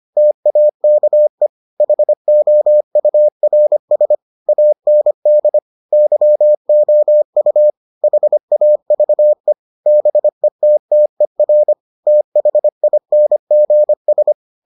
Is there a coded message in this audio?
Yes: Morse code from 0.2 to 14.3 s, heard as a keyed tone, about 25 wpm, 600 Hz. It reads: TAKE HOURS AND YOU HAVE BETTER THINGS